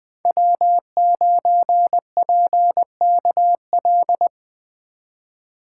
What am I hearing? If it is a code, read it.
Morse code, heard as a keyed tone, about 20 wpm, 700 Hz. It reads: W9PKL
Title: USA Call Signs - 5 Characters - 3x Character Spacing - 20wpm